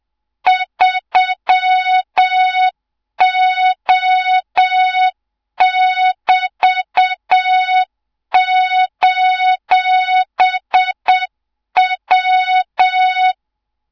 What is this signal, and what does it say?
Morse code, heard as a keyed tone, about 7 wpm, 3700 Hz, with noise. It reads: 3O<BT>:W
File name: Morze-HA2MN / hang / 24 GYAKORLAT